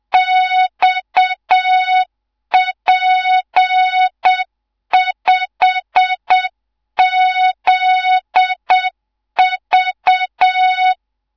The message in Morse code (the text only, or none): XP5ZV